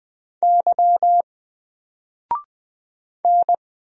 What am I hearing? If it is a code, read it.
Morse code, heard as a keyed tone, about 20 wpm, 700 Hz. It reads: Y N
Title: Single Letter-Number Alt 20wpm